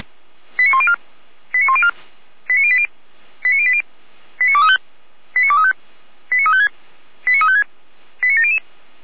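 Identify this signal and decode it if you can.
Five-tone selective call (ZVEI), heard as quick runs of five tones, tones of about 70 ms. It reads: 89104 89104 89080 89080 89235 89235 89356 89356 89700
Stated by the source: ZVEI five-tone selective call received on SDR, from the feuerwehr-orlamuende fire brigade audio server.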